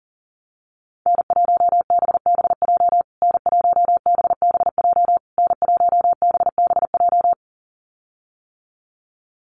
Morse code, heard as a keyed tone, about 40 wpm, 700 Hz. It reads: N1BBJ N1BBJ N1BBJ
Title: USA Call Signs Repeated 3x 40wpm